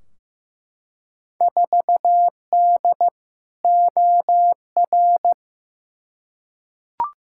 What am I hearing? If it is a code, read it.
Morse code, heard as a keyed tone, about 15 wpm, 700 Hz. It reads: 4D OR
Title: Field Day Exchanges Encore - 15wpm